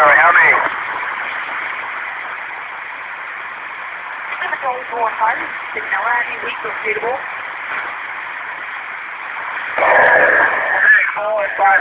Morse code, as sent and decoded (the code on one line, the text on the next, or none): none